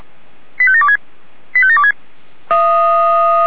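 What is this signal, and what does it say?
Five-tone selective call (ZVEI), heard as quick runs of five tones, tones of about 70 ms. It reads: 85726 85726
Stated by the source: Five-tone ZVEI selective call 85735, received by SDR.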